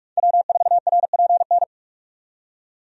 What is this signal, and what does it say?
Morse code, heard as a keyed tone, about 46 wpm, 700 Hz. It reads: W4RPN
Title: USA Call Signs - 5 Characters - 45wpm